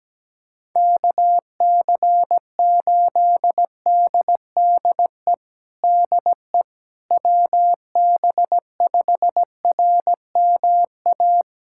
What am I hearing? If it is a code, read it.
Morse code, heard as a keyed tone, about 17 wpm, 700 Hz. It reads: KC8DDE DE WB5RMA
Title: USA Callsign DE Callsign - 6 Characters - 17wpm